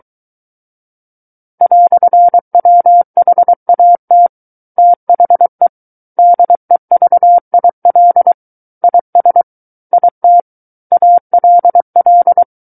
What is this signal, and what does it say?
Morse code, heard as a keyed tone, about 23 wpm, 700 Hz. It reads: "WHAT THE DEVIL IS IT ALL
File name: Book One / 14-Chapter 14 - 23 WPM